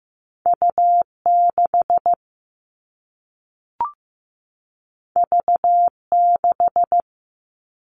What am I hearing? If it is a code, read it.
Morse code, heard as a keyed tone, about 15 wpm, 700 Hz. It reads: U6 V6
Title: All Cut Numbers - 2-Digit 15wpm